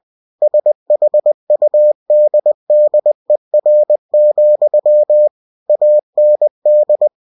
Morse code, heard as a keyed tone, about 20 wpm, 600 Hz. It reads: SHUDDER, AND
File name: cw-books / winnie-the-pooh / winnie-20wpm-page-0071